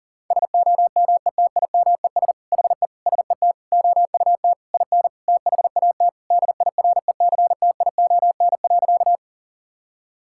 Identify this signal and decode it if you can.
Morse code, heard as a keyed tone, about 40 wpm, 700 Hz. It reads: SOMETIMES HE SET OUT IN THAT DIRECTION.
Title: Sentences from Top 1000 Words - Review Part 5 - 40wpm